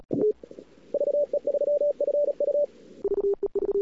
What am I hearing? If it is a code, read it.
Morse code, heard as a keyed tone, about 36 wpm, 580 Hz. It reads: VE3FU
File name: cw-sprint-k5tr-op-n5rz